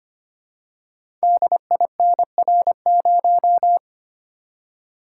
Morse code, heard as a keyed tone, about 25 wpm, 700 Hz. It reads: DINR0
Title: Code Groups - 5 Letter:Numbers - 25wpm